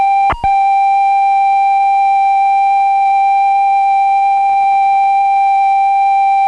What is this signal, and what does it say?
Computer data on cassette.